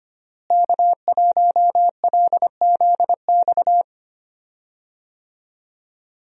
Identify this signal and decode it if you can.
Morse code, heard as a keyed tone, about 25 wpm, 700 Hz. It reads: K1LZX
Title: USA Call Signs - 5 Characters - Repeated 2x - 25wpm